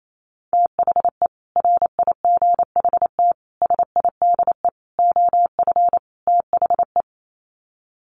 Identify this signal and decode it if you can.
Morse code, heard as a keyed tone, about 28 wpm, 700 Hz. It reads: THE RIGHT SIDE OF THE
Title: Sets of 5 Words Encore 28wpm